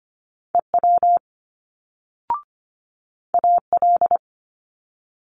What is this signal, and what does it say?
Morse code, heard as a keyed tone, about 25 wpm, 700 Hz. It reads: EW AL
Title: Multi-letter Phonemes Quick Fire 25wpm